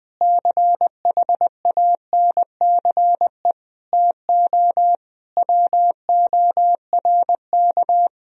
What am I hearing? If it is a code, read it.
Morse code, heard as a keyed tone, about 20 wpm, 700 Hz. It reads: CHANCE TO WORK
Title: Sentences from Top 1000 Words - Review Part 4 - 20wpm